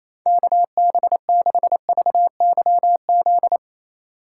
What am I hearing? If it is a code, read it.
Morse code, heard as a keyed tone, about 35 wpm, 700 Hz. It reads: KB6VYZ
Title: Speed Racing - USA 2x3 Call Signs 28wpm